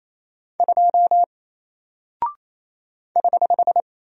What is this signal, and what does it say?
Morse code, heard as a keyed tone, about 28 wpm, 700 Hz. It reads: 2 <HH>
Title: Single Character (Letter:Number:Punctuation:Prosign) Alt 28wpm